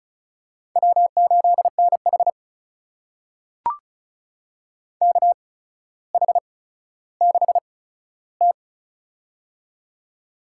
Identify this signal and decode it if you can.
Morse code, heard as a keyed tone, about 34 wpm, spaced as for 15 wpm, 700 Hz. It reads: W8NH KH6T
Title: USA Call Signs - 4 Characters - 8x Character Spacing - 35wpm